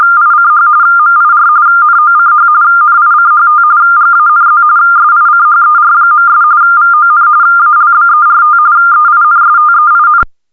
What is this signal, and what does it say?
RTTY, about 45.45 baud. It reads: ICH MIR NOCH NICHT GANZ SICHER.' VON ALBERT EINSTEIN